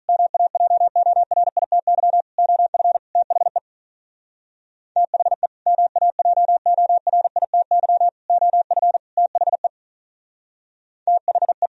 Morse code, heard as a keyed tone, about 43 wpm, 700 Hz. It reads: MAJORITY OF THE THE MAJORITY OF THE THE
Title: Speed-Racing - Sets of 4 Words 35wpm